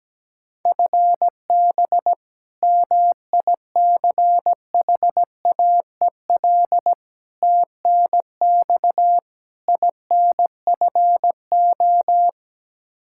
Morse code, heard as a keyed tone, about 17 wpm, 700 Hz. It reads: FB MICHAEL TNX INFO
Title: Practice QSO - Exchange 3 - More Info - 17wpm